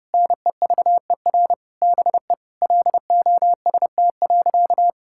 Morse code, heard as a keyed tone, about 30 wpm, 700 Hz. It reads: NEVER BE LOST.
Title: Sentences from Top 1600 Words 30wpm